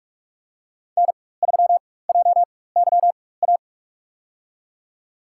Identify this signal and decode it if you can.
Morse code, heard as a keyed tone, about 44 wpm, spaced as for 21 wpm, 700 Hz. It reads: N3JYA
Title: USA Call Signs - 5 Characters - 4x Character Spacing - 45wpm